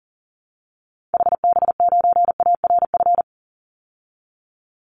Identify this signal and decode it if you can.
Morse code, heard as a keyed tone, about 40 wpm, 700 Hz. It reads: HB9ARF
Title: Call Signs Repeated 3x - 40wpm